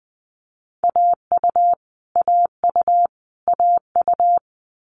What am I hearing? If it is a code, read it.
Morse code, heard as a keyed tone, about 20 wpm, 700 Hz. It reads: AU AU AU
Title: Multi-letter Phonemes Repeated 3x 20wpm